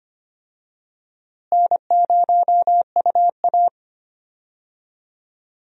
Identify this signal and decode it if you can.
Morse code, heard as a keyed tone, about 25 wpm, 700 Hz. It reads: N0UA
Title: USA Call Signs - 4 Characters - Repeated 3x - 25wpm